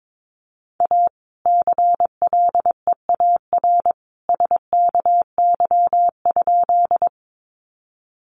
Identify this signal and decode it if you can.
Morse code, heard as a keyed tone, about 22 wpm, 700 Hz. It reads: A CLEAR SKY?
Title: Sentences from Top 1200 Words 22wpm